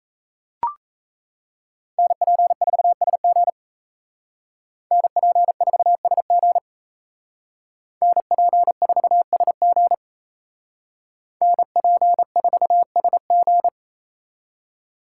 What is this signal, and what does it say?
Morse code, heard as a keyed tone, about 35 wpm, 700 Hz. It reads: NP4SG NP4SG NP4SG NP4SG